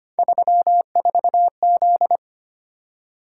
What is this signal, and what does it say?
Morse code, heard as a keyed tone, about 25 wpm, 700 Hz. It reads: 34Z
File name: Three Letter-Number Combinations - 25wpm